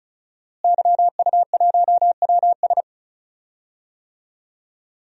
Morse code, heard as a keyed tone, about 36 wpm, 700 Hz. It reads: YU1WS
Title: DXCC-100 Call Signs - 5 Characters - Repeated 2x - 35wpm